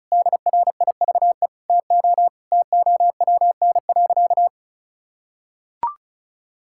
Morse code, heard as a keyed tone, about 36 wpm, 700 Hz. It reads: DRIVE TO TOWN.
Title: Sentences from Top 1000 Words - Review Part 6 - 35wpm